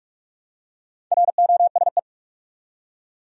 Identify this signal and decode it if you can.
Morse code, heard as a keyed tone, about 46 wpm, 700 Hz. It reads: ROSE
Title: USA Common Names - 45wpm